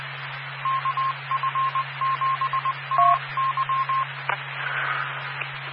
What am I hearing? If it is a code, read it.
Morse code, heard as a keyed tone, about 20 wpm, 1060 Hz. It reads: KF7AY
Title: Claremont, CA Police, Amateur Radio